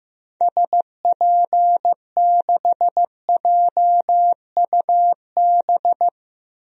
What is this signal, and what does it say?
Morse code, heard as a keyed tone, about 15 wpm, 700 Hz. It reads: SP6JUB
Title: DXCC-100 Call Signs - 6 Characters - 15wpm